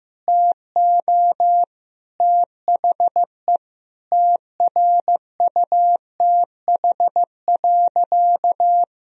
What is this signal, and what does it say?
Morse code, heard as a keyed tone, about 15 wpm, 700 Hz. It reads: TO THE TRUTH.
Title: Sentences from Top 1000 Words - Review Part 4 - 15wpm